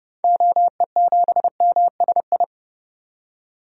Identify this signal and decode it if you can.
Morse code, heard as a keyed tone, about 30 wpm, 700 Hz. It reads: OE7MSI
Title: DXCC-100 Call Signs - 6 Characters - 30wpm